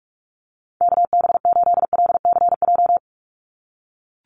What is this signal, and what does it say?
Morse code, heard as a keyed tone, about 44 wpm, 700 Hz. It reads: KB8LCJ
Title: USA Call Signs 45wpm